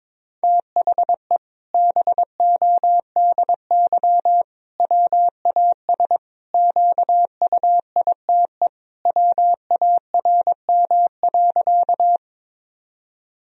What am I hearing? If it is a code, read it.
Morse code, heard as a keyed tone, about 22 wpm, 700 Hz. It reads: THE BODY WAS QUITE WARM.
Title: Sentences from Top 1000 Words - Review Part 2 - 22wpm